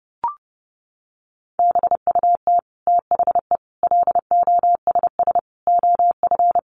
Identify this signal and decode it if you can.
Morse code, heard as a keyed tone, about 30 wpm, 700 Hz. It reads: BUT THE LOSS OF
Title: Sentences from Top 1400 Words 30wpm